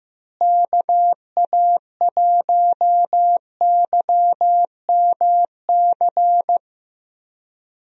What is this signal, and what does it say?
Morse code, heard as a keyed tone, about 15 wpm, 700 Hz. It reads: KA1YMC
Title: USA Call Signs - 6 Characters - Repeated 3x - 15wpm